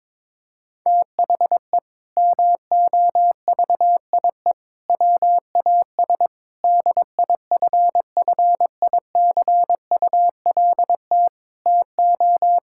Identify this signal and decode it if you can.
Morse code, heard as a keyed tone, about 22 wpm, 700 Hz. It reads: THE MOVIE WAS DIFFICULT TO
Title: Sentences from Top 1000 Words - Review Part 5 - 22wpm